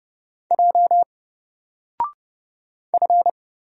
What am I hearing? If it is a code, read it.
Morse code, heard as a keyed tone, about 30 wpm, 700 Hz. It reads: J F
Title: Single Character (Letter:Number:Punctuation:Prosign) 30wpm